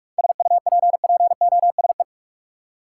Morse code, heard as a keyed tone, about 48 wpm, 700 Hz. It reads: SUPPOSE
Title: Seven Letter Words - 45wpm